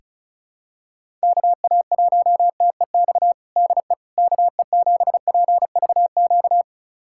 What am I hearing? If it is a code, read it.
Morse code, heard as a keyed tone, about 36 wpm, 700 Hz. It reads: KA1TEX DE KE7PVQ
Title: USA Callsign DE Callsign 35wpm